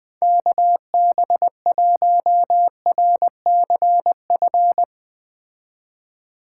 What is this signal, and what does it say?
Morse code, heard as a keyed tone, about 21 wpm, 700 Hz. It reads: KB1RCF